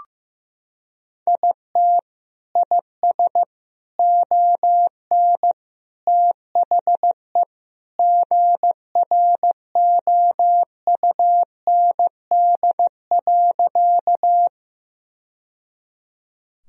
Morse code, heard as a keyed tone, about 15 wpm, 700 Hz. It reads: IT IS ON THE GROUND.
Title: Sentences from Top 1000 Words - Review Part 2 - 15wpm